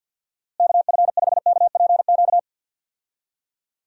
Morse code, heard as a keyed tone, about 50 wpm, 700 Hz. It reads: KF5KPQ
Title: USA Call Signs - 6 Characters - Repeated 2x - 50wpm